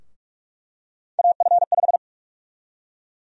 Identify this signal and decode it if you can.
Morse code, heard as a keyed tone, about 46 wpm, 700 Hz. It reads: AF5
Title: USA Call Sign Prefixes - Repeated 2x - 45wpm